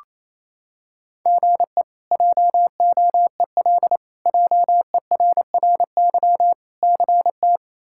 Morse code, heard as a keyed tone, about 27 wpm, 700 Hz. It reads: GE JOEL JERRY CT
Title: K1USN SST QSO Practice - 28wpm